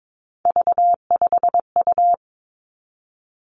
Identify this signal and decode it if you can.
Morse code, heard as a keyed tone, about 22 wpm, 700 Hz. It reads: V5U
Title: Three Challenging Letter-Number Combinations 22wpm